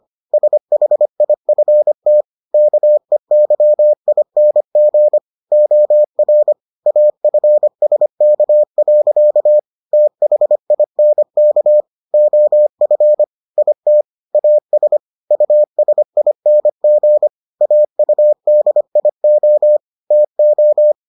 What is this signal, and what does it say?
Morse code, heard as a keyed tone, about 25 wpm, 600 Hz. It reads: SHIFT KEYING OR AFSK. THINK OF IT AS USING AUDIO TO